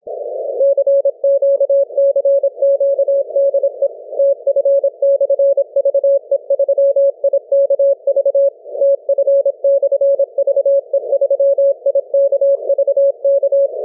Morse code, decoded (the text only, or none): CQ CQ DE TF/VE3IKV TF/VE3IKV K